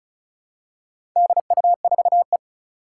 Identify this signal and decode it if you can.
Morse code, heard as a keyed tone, about 36 wpm, 690 Hz. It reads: DU4E
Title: Four Challenging Letter-Number Combinations 35wpm